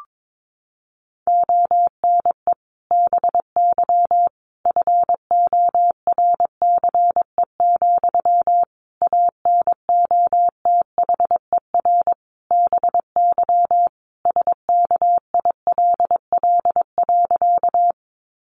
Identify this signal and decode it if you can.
Morse code, heard as a keyed tone, about 22 wpm, 700 Hz. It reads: ONE BY FORCE, ANOTHER BY SKILL.
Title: Sentences from Top 1200 Words 22wpm